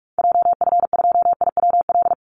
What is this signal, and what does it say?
Morse code, heard as a keyed tone, about 44 wpm, 700 Hz. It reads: JF2IWL
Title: Call Signs Repeated 2x - 45wpm